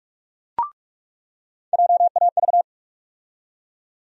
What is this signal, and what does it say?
Morse code, heard as a keyed tone, about 42 wpm, 700 Hz. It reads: JAV
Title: USA Call Sign Postfixes - 3 Letters 45wpm